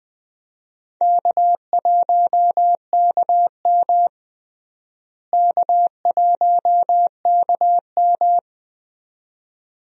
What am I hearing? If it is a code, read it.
Morse code, heard as a keyed tone, about 21 wpm, 700 Hz. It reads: K1KM K1KM
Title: USA Call Signs - 4 Characters - Repeated 2x - 20wpm